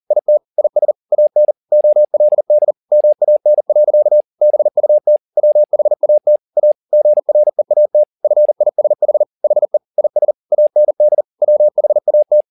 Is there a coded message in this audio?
Yes: Morse code from 0.1 to 12.4 s, heard as a keyed tone, about 40 wpm, 600 Hz. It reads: IT IS AN OLD MAN. BUT WHAT A GREAT FISH HE IS AND WHAT